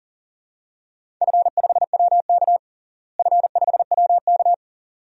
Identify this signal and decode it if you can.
Morse code, heard as a keyed tone, about 40 wpm, 700 Hz. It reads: F5WK F5WK
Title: DXCC-100 Call Signs - 4 Characters - Repeated 2x - 40wpm